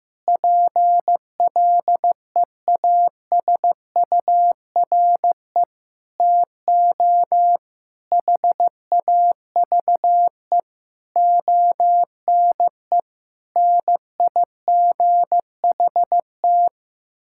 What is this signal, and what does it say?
Morse code, heard as a keyed tone, about 15 wpm, 700 Hz. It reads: PLEASURE TO HAVE ONE NIGHT
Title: Sentences from Top 1600 Words 15wpm